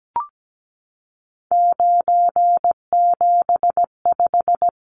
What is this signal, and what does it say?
Morse code, heard as a keyed tone, about 17 wpm, 700 Hz. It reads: 975
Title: Numbers - 3-Digit 17wpm